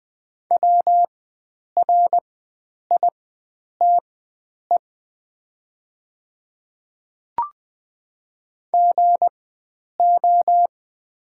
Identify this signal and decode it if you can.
Morse code, heard as a keyed tone, about 20 wpm, spaced as for 9 wpm, 700 Hz. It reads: WRITE GO
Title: 4x Character Spacing - Top 100 Words - 20wpm